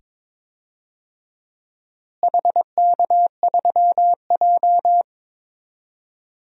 Morse code, heard as a keyed tone, about 22 wpm, 700 Hz. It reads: HK3J
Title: DXCC-100 Call Signs - 4 Characters - Repeated 2x - 22wpm